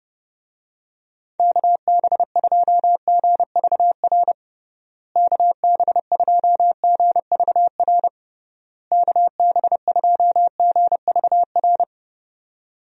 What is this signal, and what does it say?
Morse code, heard as a keyed tone, about 31 wpm, 700 Hz. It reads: KB2GVR KB2GVR KB2GVR
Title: USA Call Signs - 6 Characters - Repeated 3x - 30wpm